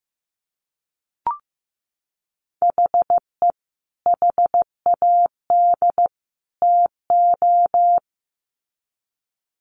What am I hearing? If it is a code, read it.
Morse code, heard as a keyed tone, about 15 wpm, 700 Hz. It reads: HE HAD TO
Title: Sets of 3 Words 15wpm